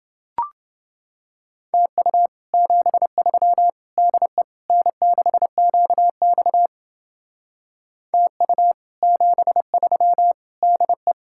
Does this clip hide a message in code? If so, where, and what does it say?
Yes: Morse code from 1.7 to 11.1 s, heard as a keyed tone, about 29 wpm, 700 Hz. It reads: TU 73 DE N6QX TU 73 DE